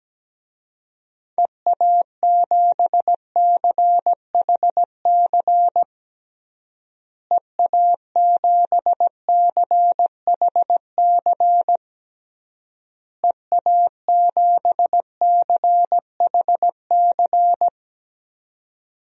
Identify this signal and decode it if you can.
Morse code, heard as a keyed tone, about 17 wpm, 700 Hz. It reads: EA7CHC EA7CHC EA7CHC
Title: DXCC-100 Call Signs - 6 Characters - Repeated 3x - 17wpm